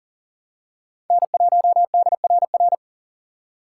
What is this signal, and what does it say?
Morse code, heard as a keyed tone, about 39 wpm, 700 Hz. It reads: N1DRR
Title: USA Call Signs - 5 Characters - 2x Character Spacing - 40wpm